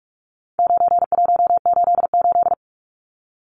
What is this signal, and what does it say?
Morse code, heard as a keyed tone, about 44 wpm, 700 Hz. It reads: 9188
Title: Numbers - 4-Digit 45wpm